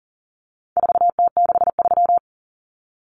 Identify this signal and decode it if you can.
Morse code, heard as a keyed tone, about 40 wpm, 700 Hz. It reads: 4T63
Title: Common Cut Numbers - 4-Digit 40wpm